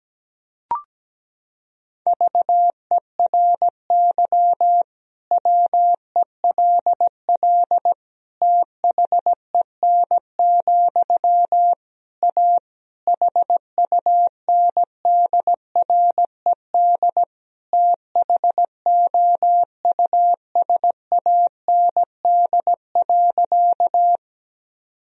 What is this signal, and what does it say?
Morse code, heard as a keyed tone, about 17 wpm, 700 Hz. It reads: VERY WELL THEN, A HUNDRED THOUSAND.
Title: Sentences from Top 1000 Words - Review Part 3 - 17wpm